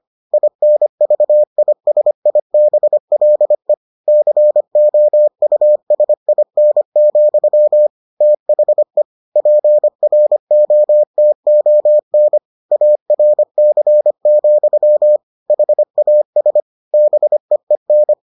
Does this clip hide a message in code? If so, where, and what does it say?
Yes: Morse code from 0.3 to 18.1 s, heard as a keyed tone, about 25 wpm, 600 Hz. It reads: INVISIBLE COUSIN, THE PROTON ARC, HAS BEEN